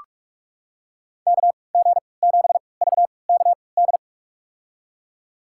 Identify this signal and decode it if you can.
Morse code, heard as a keyed tone, about 47 wpm, spaced as for 26 wpm, 700 Hz. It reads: KG7VKD
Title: USA Call Signs - 3x Character Spacing - 45wpm